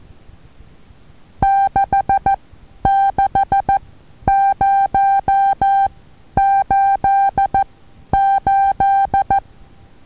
Morse code, heard as a keyed tone, about 14 wpm, spaced as for 10 wpm, 790 Hz. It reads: 66088
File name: Taviro HA6VA A